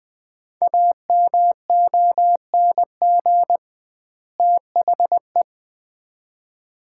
Morse code, heard as a keyed tone, about 20 wpm, 700 Hz. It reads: AMONG THE
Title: 2x Word-Spacing - Sets of 2 Words - 20wpm